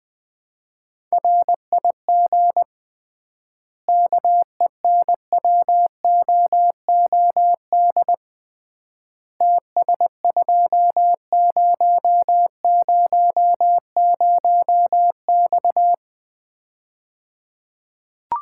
Morse code, heard as a keyed tone, about 20 wpm, 700 Hz. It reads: RIG KENWOOD TS2000X
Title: Common QSO Phrases - 3x Word-Spacing - 20wpm